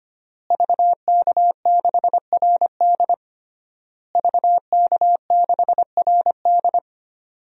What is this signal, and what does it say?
Morse code, heard as a keyed tone, about 25 wpm, 700 Hz. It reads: VK6RD VK6RD
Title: DXCC-100 Call Signs - 5 Characters - Repeated 3x - 25wpm